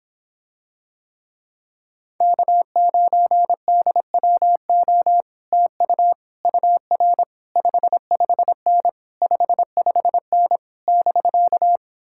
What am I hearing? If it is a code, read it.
Morse code, heard as a keyed tone, about 26 wpm, 700 Hz. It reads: K9DWO TU UR 55N 55N <BK>